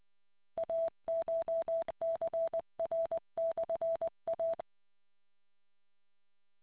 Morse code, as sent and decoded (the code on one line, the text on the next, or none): .- ---- -.-. .-. -..-. .-
ACHCR/A